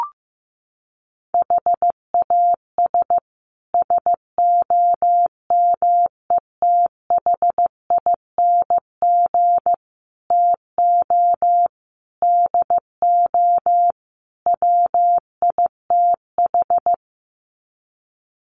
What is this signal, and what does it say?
Morse code, heard as a keyed tone, about 15 wpm, 700 Hz. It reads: HAS SOMETHING TO DO WITH
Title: Sets of 5 Words Encore 15wpm